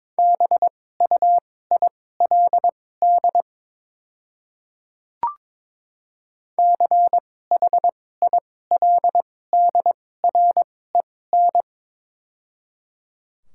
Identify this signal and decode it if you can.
Morse code, heard as a keyed tone, about 22 wpm, spaced as for 16 wpm, 700 Hz. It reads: BUILD CHILDREN
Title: 2x Character Spacing - Top 300 Words - 22wpm